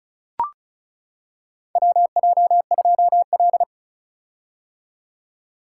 Morse code, heard as a keyed tone, about 34 wpm, 700 Hz. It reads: WJ2L
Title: K1USN SST QSO Practice - 35wpm